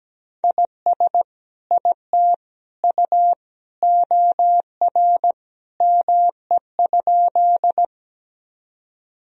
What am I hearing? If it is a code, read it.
Morse code, heard as a keyed tone, about 17 wpm, 700 Hz. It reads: IS IT U OR ME?
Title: Sentences from Top 200 Words and Abbreviations 17wpm